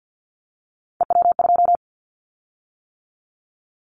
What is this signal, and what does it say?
Morse code, heard as a keyed tone, about 50 wpm, 700 Hz. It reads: EW2
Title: Callsign Prefixes for Top 50 DXCC - 50wpm